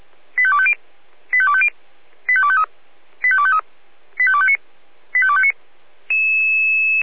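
Five-tone selective call (ZVEI), heard as quick runs of five tones, tones of about 70 ms. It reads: 85250 85250 85253 85253 85259 85259
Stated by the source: ZVEI five-tone selective call 85252, received via SDR.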